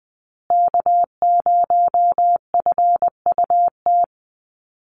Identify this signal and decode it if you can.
Morse code, heard as a keyed tone, about 20 wpm, 700 Hz. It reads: K0FUT
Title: USA Call Signs Repeated 3x 20wpm